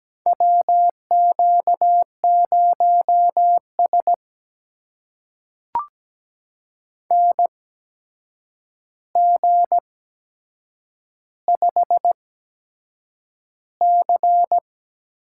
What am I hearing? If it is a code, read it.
Morse code, heard as a keyed tone, about 17 wpm, spaced as for 7 wpm, 700 Hz. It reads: WQ0S NG5C
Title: USA Call Signs - 4 Characters - 8x Character Spacing - 17wpm